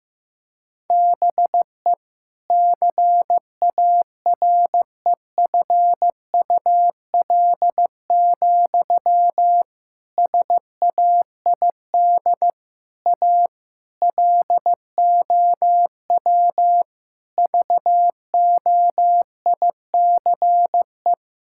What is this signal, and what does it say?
Morse code, heard as a keyed tone, about 15 wpm, 700 Hz. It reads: BE CAREFUL, SAID A LOW VOICE
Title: Sentences from Top 1600 Words 15wpm